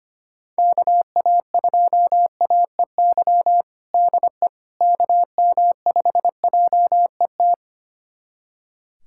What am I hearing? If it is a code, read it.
Morse code, heard as a keyed tone, about 25 wpm, 700 Hz. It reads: KA2AEY DE KM5JET
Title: USA Callsign DE Callsign - 6 Characters - 25wpm